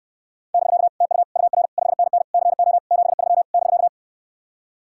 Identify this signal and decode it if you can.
Morse code, heard as a keyed tone, about 67 wpm, 700 Hz. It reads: <BK> TU UR 5NN CO 73 <BK>